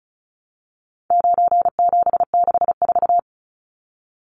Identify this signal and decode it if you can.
Morse code, heard as a keyed tone, about 34 wpm, 700 Hz. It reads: 9764
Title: Numbers - 4-Digit 35wpm